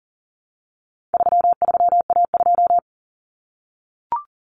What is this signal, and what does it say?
Morse code, heard as a keyed tone, about 40 wpm, 700 Hz. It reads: 33A2
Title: Common Cut Numbers - 4-Digit 40wpm